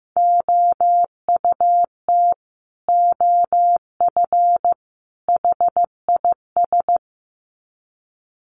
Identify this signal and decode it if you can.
Morse code, heard as a keyed tone, about 15 wpm, 700 Hz. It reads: OUT OF HIS
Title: Sets of 3 Words 15wpm